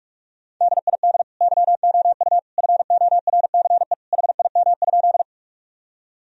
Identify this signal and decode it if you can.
Morse code, heard as a keyed tone, about 46 wpm, 700 Hz. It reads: DID YOU FORCE HIM?
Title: Sentences from Top 1000 Words - Review Part 3 - 45wpm